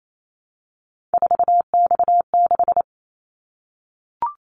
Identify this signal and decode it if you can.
Morse code, heard as a keyed tone, about 28 wpm, 700 Hz. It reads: 4X6
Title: Callsign Prefixes for Top 50 DXCC - 28wpm